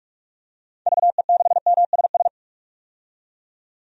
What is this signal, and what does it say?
Morse code, heard as a keyed tone, about 46 wpm, 700 Hz. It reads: VE6MSS
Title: DXCC-100 Call Signs - 6 Characters - 45wpm